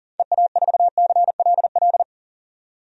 Morse code, heard as a keyed tone, about 40 wpm, 700 Hz. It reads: EA4CLL
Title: DXCC-100 Call Signs - 6 Characters - 40wpm